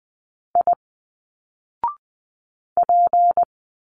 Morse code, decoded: I P